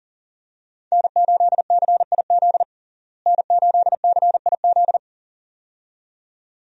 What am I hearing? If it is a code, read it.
Morse code, heard as a keyed tone, about 37 wpm, 700 Hz. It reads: N8CIZ N8CIZ
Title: USA Call Signs - 5 Characters - Repeated 2x - 40wpm